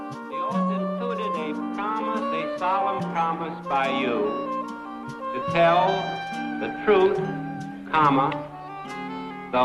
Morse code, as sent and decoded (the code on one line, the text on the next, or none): none